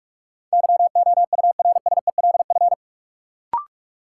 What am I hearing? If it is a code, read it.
Morse code, heard as a keyed tone, about 44 wpm, 700 Hz. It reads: YOURSELF
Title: Sentences from Top 600 Words - New Word Drill - 45wpm